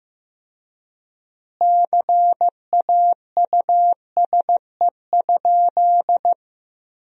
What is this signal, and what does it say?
Morse code, heard as a keyed tone, about 15 wpm, 700 Hz. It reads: CAUSE?
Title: 3x Word-Spacing - Sentences from Top 200 Words - 15wpm